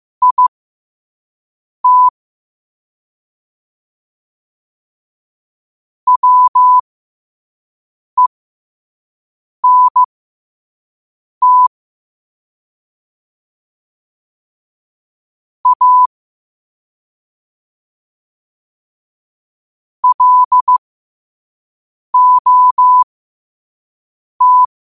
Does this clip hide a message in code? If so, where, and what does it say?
Yes: Morse code from 0.2 to 24.6 s, heard as a keyed tone, about 15 wpm, spaced as for 5 wpm, 1000 Hz. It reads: IT WENT A LOT